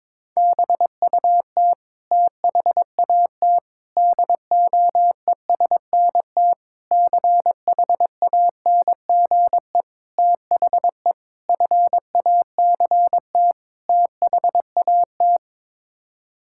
Morse code, heard as a keyed tone, about 22 wpm, 700 Hz. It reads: BUT THAT DOESNT CHANGE THE FACT THAT
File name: practice / 22wpm / Sets of 7 Words - 22wpm